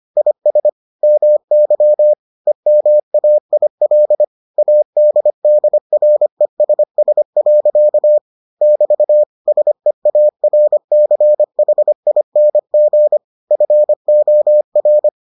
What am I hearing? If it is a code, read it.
Morse code, heard as a keyed tone, about 25 wpm, 600 Hz. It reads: IS MY EMAIL ADDRESS. <BT> SEARCHING FOR